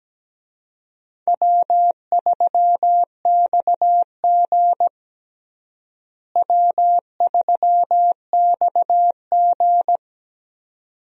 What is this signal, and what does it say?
Morse code, heard as a keyed tone, about 17 wpm, 700 Hz. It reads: W3XG W3XG
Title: USA Call Signs - 4 Characters - Repeated 2x - 17wpm